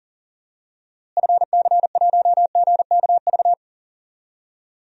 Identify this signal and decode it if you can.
Morse code, heard as a keyed tone, about 37 wpm, 700 Hz. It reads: FC1GKV
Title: DXCC-100 Call Signs - 6 Characters - Repeated 2x - 40wpm